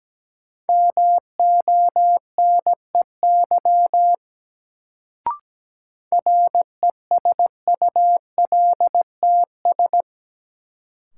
Morse code, heard as a keyed tone, about 17 wpm, 700 Hz. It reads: MONEY RESULTS
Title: Sets of 4 Words - New Word Drill - 17wpm